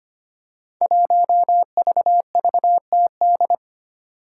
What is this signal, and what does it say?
Morse code, heard as a keyed tone, about 25 wpm, 700 Hz. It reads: 1VVTD
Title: Five Challenging Letter-Number Combinations 25wpm